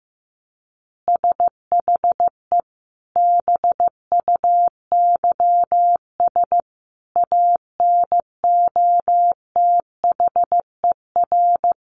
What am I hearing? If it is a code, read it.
Morse code, heard as a keyed tone, about 14 wpm, 700 Hz. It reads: SHE BUYS ANOTHER
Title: Easy Phrases 15wpm